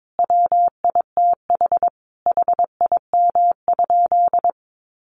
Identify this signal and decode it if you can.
Morse code, heard as a keyed tone, about 22 wpm, 700 Hz. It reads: WITH HIM?
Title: Sentences from Top 1400 Words 22wpm